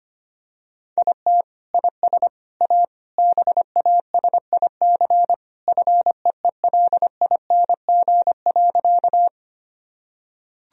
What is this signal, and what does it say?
Morse code, heard as a keyed tone, about 25 wpm, 700 Hz. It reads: IT IS A BASIC FEELING.
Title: Sentences from Top 1000 Words - Review Part 3 - 25wpm